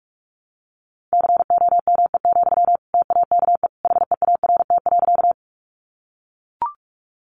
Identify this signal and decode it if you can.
Morse code, heard as a keyed tone, about 44 wpm, 700 Hz. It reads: COME, TAKE HEART.
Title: Sentences from Top 1200 Words 45wpm